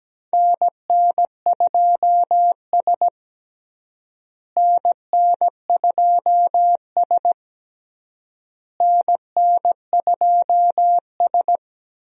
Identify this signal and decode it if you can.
Morse code, heard as a keyed tone, about 17 wpm, 700 Hz. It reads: NN2S NN2S NN2S
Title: USA Call Signs - 4 Characters - Repeated 3x - 17wpm